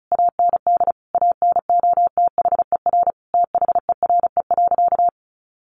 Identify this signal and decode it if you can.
Morse code, heard as a keyed tone, about 34 wpm, 700 Hz. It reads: AND ANOTHER THERE.
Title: Sentences from Top 1400 Words 35wpm